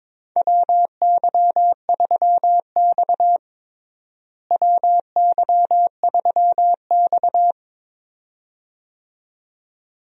Morse code, heard as a keyed tone, about 22 wpm, 700 Hz. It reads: WY3X WY3X
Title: USA Call Signs - 4 Characters - Repeated 2x - 22wpm